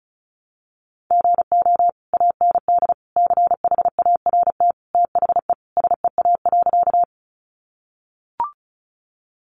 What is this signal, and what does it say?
Morse code, heard as a keyed tone, about 34 wpm, 700 Hz. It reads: GO AND CHART THE SEA.
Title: Sentences from Top 1300 Words 35wpm